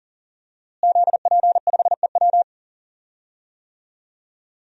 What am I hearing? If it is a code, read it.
Morse code, heard as a keyed tone, about 38 wpm, 700 Hz. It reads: ZP5EW
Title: DXCC-100 Call Signs - 5 Characters - Repeated 2x - 40wpm